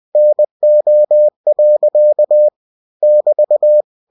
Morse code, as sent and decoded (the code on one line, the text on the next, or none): -. --- .-.-.- / -...-
NO. <BT>